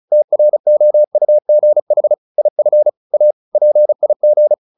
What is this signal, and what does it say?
Morse code, heard as a keyed tone, about 34 wpm, 600 Hz. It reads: TROUGH IF A PIG
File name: the old man and the sea 2 35wpm